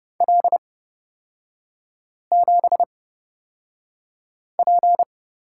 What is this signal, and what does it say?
Morse code, heard as a keyed tone, about 30 wpm, spaced as for 5 wpm, 700 Hz. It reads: L7P
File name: Lesson 36e - BT - Three Characters 30wpm